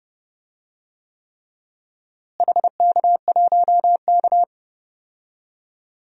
Morse code, heard as a keyed tone, about 31 wpm, 700 Hz. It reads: HK1K
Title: DXCC-100 Call Signs - 4 Characters - Repeated 3x - 30wpm